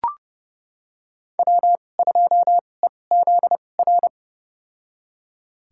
Morse code, heard as a keyed tone, about 30 wpm, spaced as for 22 wpm, 700 Hz. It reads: W2EZR
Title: USA Call Signs - 5 Characters - 2x Character Spacing - 30wpm